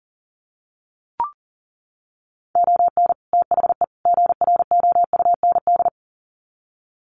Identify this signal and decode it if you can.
Morse code, heard as a keyed tone, about 40 wpm, 700 Hz. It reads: ON THE GROUND
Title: Sets of 3 Words 40wpm